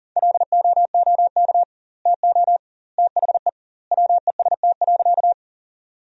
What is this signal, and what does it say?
Morse code, heard as a keyed tone, about 40 wpm, 700 Hz. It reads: LOOK TO THE WEST.
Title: 2x Word-Spacing - Sentences from Top 400 Words - 40wpm